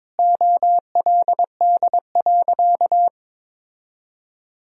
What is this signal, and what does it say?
Morse code, heard as a keyed tone, about 22 wpm, 700 Hz. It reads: OLD.
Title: Sentences from Top 1000 Words - Review Part 3 - 22wpm